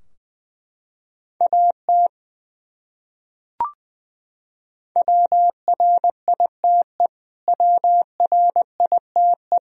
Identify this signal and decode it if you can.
Morse code, heard as a keyed tone, about 20 wpm, 700 Hz. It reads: AT WRITE WRITE
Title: Top 100 Words Repeated 20wpm